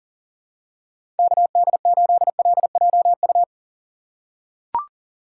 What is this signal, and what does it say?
Morse code, heard as a keyed tone, about 40 wpm, 700 Hz. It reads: KD8LJU
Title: USA Call Signs - 6x Character Spacing - 40wpm